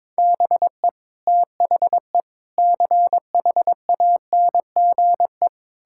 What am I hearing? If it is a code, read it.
Morse code, heard as a keyed tone, about 22 wpm, 700 Hz. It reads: BE THE CHANGE
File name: Sets of 9 Words - 22wpm